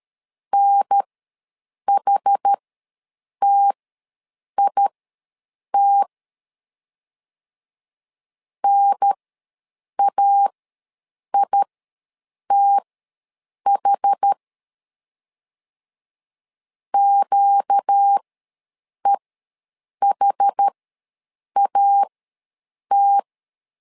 Morse code, heard as a keyed tone, about 13 wpm, spaced as for 7 wpm, 800 Hz. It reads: NHTIT NAITH QEHAT
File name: LEZIONI / 4 Q ws 12wpm 1912